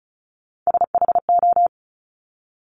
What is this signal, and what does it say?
Morse code, heard as a keyed tone, about 36 wpm, 700 Hz. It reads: SHO
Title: Top Three Letter Combinations in Words 35wpm